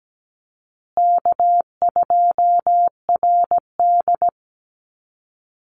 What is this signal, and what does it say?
Morse code, heard as a keyed tone, about 17 wpm, 700 Hz. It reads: K2RD
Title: Easy Call Signs Repeated 2x - 17wpm